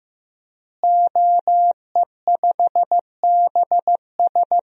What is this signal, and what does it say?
Morse code, heard as a keyed tone, about 15 wpm, 700 Hz. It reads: OE5BS
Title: DXCC-100 Call Signs - 5 Characters - Repeated 3x - 15wpm